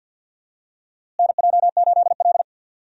Morse code, heard as a keyed tone, about 52 wpm, 700 Hz. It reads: NJ8L
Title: USA Call Signs - 4 Characters - Repeated 3x - 50wpm